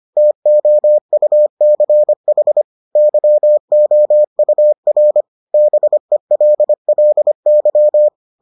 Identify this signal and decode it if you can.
Morse code, heard as a keyed tone, about 25 wpm, 600 Hz. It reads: TOUCH YOUR BELLY